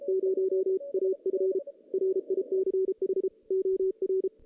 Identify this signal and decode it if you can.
Morse code, heard as a keyed tone, about 33 wpm, 380 Hz. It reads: 0 AF RICH OR
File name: 2016 JAN CW NAQP / 2016 Jan naqp k5ot - 0-7